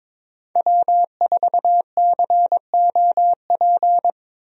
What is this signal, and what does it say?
Morse code, heard as a keyed tone, about 22 wpm, 700 Hz. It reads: W4COP
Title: USA Call Signs - 5 Characters - 22wpm